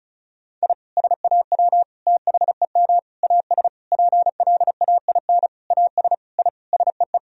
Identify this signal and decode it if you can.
Morse code, heard as a keyed tone, about 36 wpm, 700 Hz. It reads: I SAW THEM AS PLAIN AS I SEE
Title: Sentences from Top 1000 Words - Review Part 2 - 35wpm